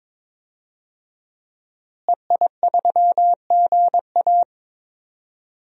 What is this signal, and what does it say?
Morse code, heard as a keyed tone, about 22 wpm, 700 Hz. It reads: EI3GA
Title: DXCC-100 Call Signs - 5 Characters - Repeated 2x - 22wpm